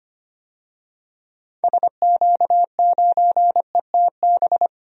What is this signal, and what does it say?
Morse code, heard as a keyed tone, about 25 wpm, 700 Hz. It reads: SQ9ETB
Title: DXCC-100 Call Signs - 6 Characters - Repeated 2x - 25wpm